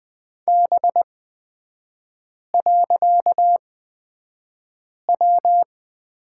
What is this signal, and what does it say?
Morse code, heard as a keyed tone, about 20 wpm, spaced as for 5 wpm, 700 Hz. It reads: B.W
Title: 255 Lesson 41e - Period - Three Characters 20wpm